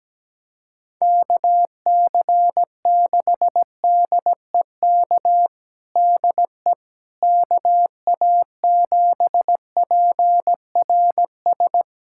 Morse code, heard as a keyed tone, about 17 wpm, 700 Hz. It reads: KC6DEK DE KA7PRS
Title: USA Callsign DE Callsign - 6 Characters - 17wpm